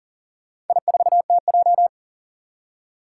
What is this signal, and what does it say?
Morse code, heard as a keyed tone, about 41 wpm, 700 Hz. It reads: I4TJ
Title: Four Challenging Letter-Number Combinations 40wpm